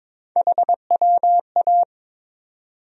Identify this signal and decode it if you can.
Morse code, heard as a keyed tone, about 22 wpm, 700 Hz. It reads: HWA
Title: USA Call Sign Postfixes - 3 Letters 22wpm